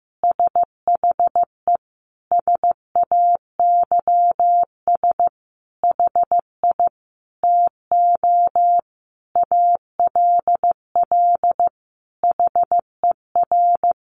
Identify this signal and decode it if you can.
Morse code, heard as a keyed tone, about 15 wpm, 700 Hz. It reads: SHE SAYS HI TO ALL HER
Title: Easy Phrases 15wpm